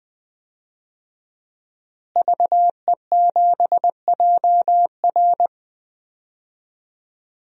Morse code, heard as a keyed tone, about 20 wpm, 700 Hz. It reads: VE7JR
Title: DXCC-100 Call Signs - 5 Characters - Repeated 2x - 20wpm